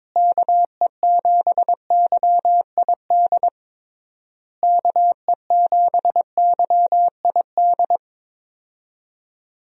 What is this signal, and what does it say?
Morse code, heard as a keyed tone, about 22 wpm, 700 Hz. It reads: KE7YID KE7YID
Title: USA Call Signs - 6 Characters - Repeated 3x - 22wpm